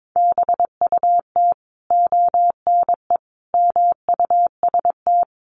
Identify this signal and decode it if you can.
Morse code, heard as a keyed tone, about 22 wpm, 700 Hz. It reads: BUT ONE MUST
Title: Sentences from Top 1500 Words 22wpm